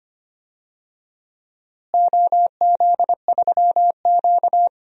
Morse code, heard as a keyed tone, about 25 wpm, 700 Hz. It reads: OZ3Q
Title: DXCC-100 Call Signs - 4 Characters - Repeated 3x - 25wpm